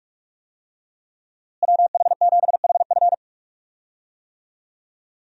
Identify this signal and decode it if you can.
Morse code, heard as a keyed tone, about 42 wpm, 700 Hz. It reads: WH7HF
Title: USA Call Signs - 5 Characters - Repeated 3x - 45wpm